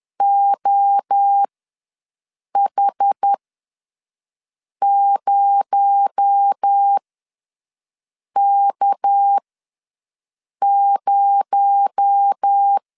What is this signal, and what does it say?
Morse code, heard as a keyed tone, about 10 wpm, spaced as for 5 wpm, 800 Hz. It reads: OH0K0
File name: LEZIONI / 6 PKX0 / 6 PKX0 ws 10wpm 4220